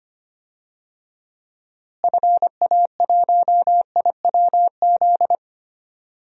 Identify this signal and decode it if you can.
Morse code, heard as a keyed tone, about 25 wpm, 700 Hz. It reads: FA1IWZ
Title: DXCC-100 Call Signs - 6 Characters - Repeated 3x - 25wpm